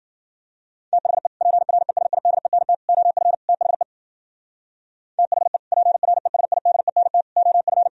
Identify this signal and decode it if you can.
Morse code, heard as a keyed tone, about 50 wpm, 700 Hz. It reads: THE PRESIDENT OF THE THE PRESIDENT OF